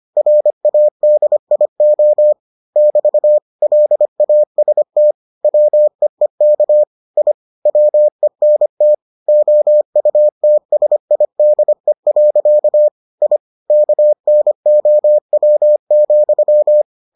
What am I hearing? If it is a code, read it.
Morse code, heard as a keyed tone, about 25 wpm, 600 Hz. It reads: RADIO <BT> LAST WEEK I WENT OUTSIDE. I KNOW,